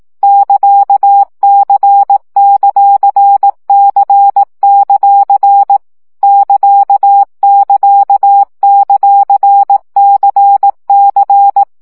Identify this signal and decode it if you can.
Morse code, heard as a keyed tone, about 18 wpm, 800 Hz. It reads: <KA>C;C; <KA><KA>;CC
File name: CW gyakorlas c ct pontosvesszo 15 baud